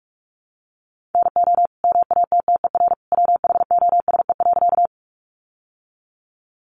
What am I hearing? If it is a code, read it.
Morse code, heard as a keyed tone, about 44 wpm, 700 Hz. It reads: NO MATTER WHOSE.
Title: Sentences from Top 1200 Words 45wpm